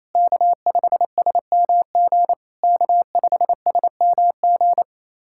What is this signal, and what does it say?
Morse code, heard as a keyed tone, about 28 wpm, 700 Hz. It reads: K5SMG K5SMG
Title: SOTA QSO Practice 28wpm